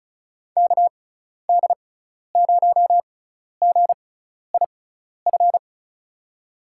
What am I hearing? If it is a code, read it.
Morse code, heard as a keyed tone, about 36 wpm, spaced as for 12 wpm, 700 Hz. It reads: KD0GIF
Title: USA Call Signs - 6x Character Spacing - 35wpm